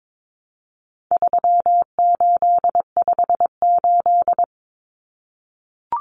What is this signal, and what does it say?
Morse code, heard as a keyed tone, about 22 wpm, 700 Hz. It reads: 3858
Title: Common Cut Numbers - 4-Digit 22wpm